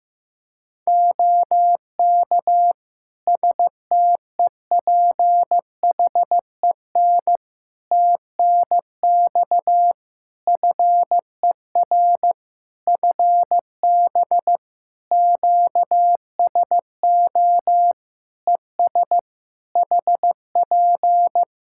Morse code, heard as a keyed tone, about 15 wpm, 700 Hz. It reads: OK STEPHEN TNX FER FB QSO ES HP
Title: Practice QSO - Exchange 4 - Ending - 15wpm